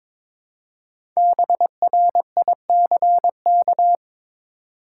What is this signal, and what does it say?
Morse code, heard as a keyed tone, about 22 wpm, 700 Hz. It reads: BRICK
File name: 5 Letter Words - First 3 Letters Given - 22wpm